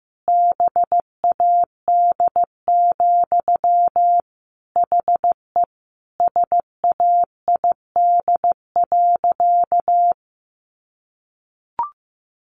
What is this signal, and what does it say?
Morse code, heard as a keyed tone, about 15 wpm, 700 Hz. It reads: BAD, HE SAID.
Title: Sentences from Top 700 Words 15wpm